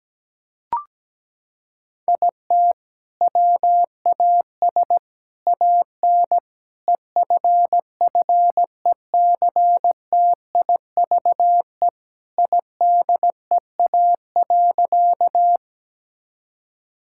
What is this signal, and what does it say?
Morse code, heard as a keyed tone, about 17 wpm, 700 Hz. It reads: IT WAS AN EFFECTIVE IDEA.
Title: Sentences from Top 1000 Words - Review Part 6 - 17wpm